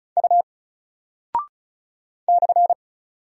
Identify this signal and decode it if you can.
Morse code, heard as a keyed tone, about 36 wpm, 700 Hz. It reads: U /
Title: Single Character (Letter:Number:Punctuation:Prosign) Alt 35wpm